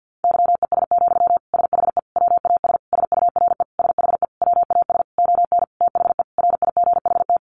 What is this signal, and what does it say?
Morse code, heard as a keyed tone, about 50 wpm, 700 Hz. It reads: YES, SHE WAS SURE SHE WAS ON THE RIGHT
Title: Sentences from Top 1100 Words 50wpm